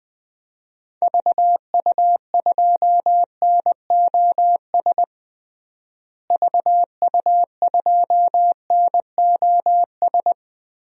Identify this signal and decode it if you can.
Morse code, heard as a keyed tone, about 20 wpm, 700 Hz. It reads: VU2NOS VU2NOS
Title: DXCC-100 Call Signs - 6 Characters - Repeated 2x - 20wpm